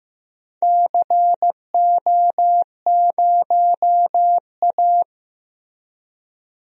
Morse code, heard as a keyed tone, about 15 wpm, 700 Hz. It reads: CO0A
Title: DXCC-100 Call Signs - 4 Characters - Repeated 2x - 15wpm